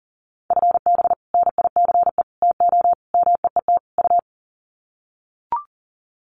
Morse code, heard as a keyed tone, about 40 wpm, 700 Hz. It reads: FB NICE TO MEET U
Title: Common QSO Phrases 40wpm